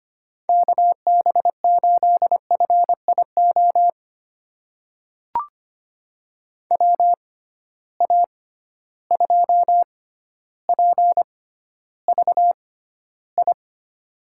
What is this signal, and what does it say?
Morse code, heard as a keyed tone, about 25 wpm, spaced as for 13 wpm, 700 Hz. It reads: KB8FIO WA2PVI
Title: USA Call Signs - 6 Characters - 6x Character Spacing - 25wpm